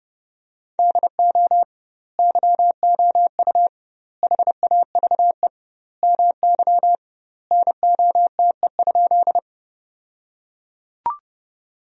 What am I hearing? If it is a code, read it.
Morse code, heard as a keyed tone, about 32 wpm, 700 Hz. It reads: DO YOU HAVE MY NOTE?
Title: 2x Word-Spacing - Sentences from Top 500 Words - 30wpm